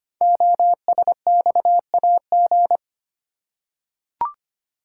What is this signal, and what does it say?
Morse code, heard as a keyed tone, about 25 wpm, 700 Hz. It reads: OSXAG
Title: Code Groups - 5 Letters - 25wpm